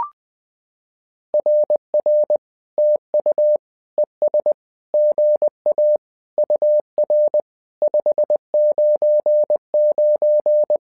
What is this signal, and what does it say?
Morse code, heard as a keyed tone, about 20 wpm, 600 Hz. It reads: RR TU ES GA UR 599
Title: SOTA QSO Practice 20wpm